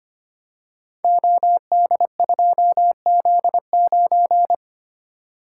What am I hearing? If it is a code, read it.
Morse code, heard as a keyed tone, about 25 wpm, 700 Hz. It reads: OD2Z9
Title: Code Groups - 5 Letter:Numbers - 25wpm